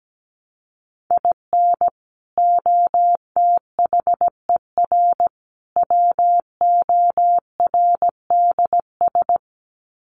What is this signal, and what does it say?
Morse code, heard as a keyed tone, about 17 wpm, 700 Hz. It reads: IN OTHER WORDS
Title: Sets of 3 Words 17wpm